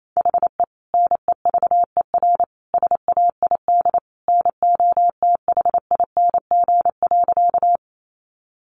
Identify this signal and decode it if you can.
Morse code, heard as a keyed tone, about 28 wpm, 700 Hz. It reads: HE NEVER SAID NOTHING.
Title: Sentences from Top 500 Words 28wpm